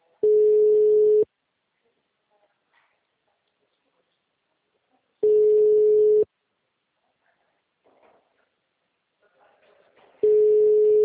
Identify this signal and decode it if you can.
Ringback tone (Europe, 425 Hz), with speech.